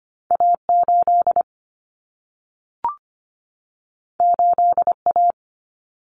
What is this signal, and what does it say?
Morse code, heard as a keyed tone, about 25 wpm, 700 Hz. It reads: A8 8A
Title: Common Cut Numbers - 2-Digit 25wpm